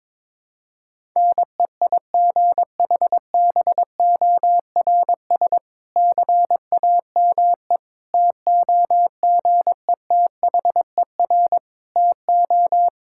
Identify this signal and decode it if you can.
Morse code, heard as a keyed tone, about 22 wpm, 700 Hz. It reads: NEIGHBORS CAME TOGETHER TO
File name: Difficult Sentences with Hints - 22wpm